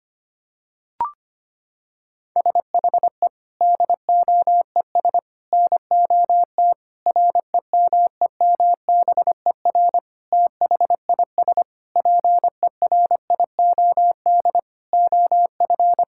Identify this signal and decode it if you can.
Morse code, heard as a keyed tone, about 25 wpm, 700 Hz. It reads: SHE DOES NOT REMEMBER THIS PERIOD OF
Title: Sentences from Top 1000 Words - Review Part 5 - 25wpm